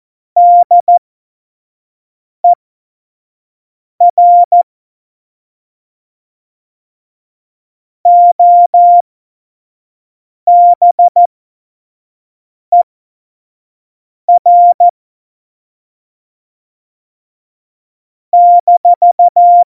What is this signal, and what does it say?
Morse code, heard as a keyed tone, about 14 wpm, spaced as for 5 wpm, 700 Hz. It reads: DER OBER -
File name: Morsekurs / Witze / Witz 10. September 2019, 25 F70